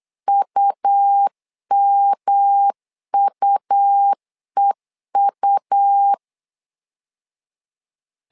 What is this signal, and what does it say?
Morse code, heard as a keyed tone, about 8 wpm, 800 Hz. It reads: UMUEU
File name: LEZIONI / 3 MWRO / 3 M 8wpm 2300